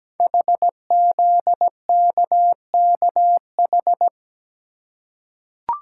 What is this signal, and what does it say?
Morse code, heard as a keyed tone, about 17 wpm, 700 Hz. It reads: HZKKH
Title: Code Groups - 5 Letters - 17wpm